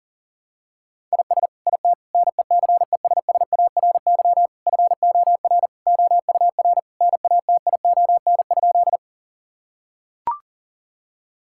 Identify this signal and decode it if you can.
Morse code, heard as a keyed tone, about 40 wpm, 700 Hz. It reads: IS IT NECESSARY FOR OUR NATION?
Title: Sentences from Top 1000 Words - Review Part 6 - 40wpm